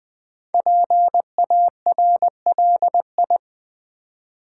Morse code, heard as a keyed tone, about 20 wpm, 700 Hz. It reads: PARLI
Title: Build A Word - 5 Letters At A Time - 10-Letter Words - 20wpm